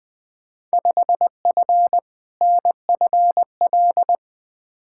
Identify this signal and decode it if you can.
Morse code, heard as a keyed tone, about 20 wpm, 700 Hz. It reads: 5F NFL
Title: Field Day Exchanges Encore - 20wpm